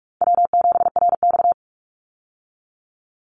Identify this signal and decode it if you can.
Morse code, heard as a keyed tone, about 44 wpm, 700 Hz. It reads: W7RX
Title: Call Signs Repeated 2x - 45wpm